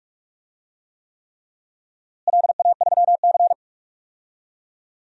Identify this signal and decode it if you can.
Morse code, heard as a keyed tone, about 46 wpm, 700 Hz. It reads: LA3C
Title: DXCC-100 Call Signs - 4 Characters - Repeated 3x - 45wpm